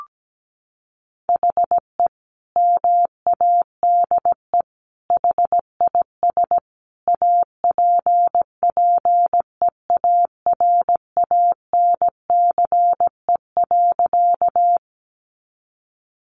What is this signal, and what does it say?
Morse code, heard as a keyed tone, about 17 wpm, 700 Hz. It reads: HE MADE HIS APPEARANCE.
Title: Sentences from Top 1100 Words 17wpm